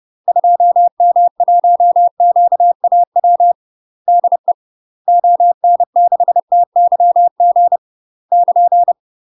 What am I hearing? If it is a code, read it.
Morse code, heard as a keyed tone, about 30 wpm, 700 Hz. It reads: 2M1QAW DE ON6TYG <KN>
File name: UK MASTER QSOs TRAINING / 30WPM / UK MASTER QSOs 0049